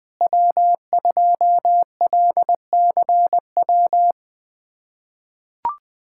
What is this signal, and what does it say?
Morse code, heard as a keyed tone, about 20 wpm, 700 Hz. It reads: W2LCW
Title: K1USN SST QSO Practice - 20wpm